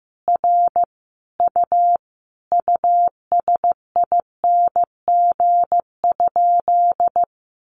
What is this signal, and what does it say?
Morse code, heard as a keyed tone, about 15 wpm, 700 Hz. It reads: R U USING?
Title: Common QSO Phrases 15wpm